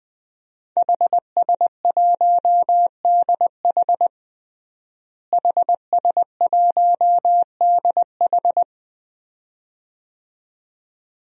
Morse code, heard as a keyed tone, about 20 wpm, 700 Hz. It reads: HS1DH HS1DH
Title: DXCC-100 Call Signs - 5 Characters - Repeated 2x - 20wpm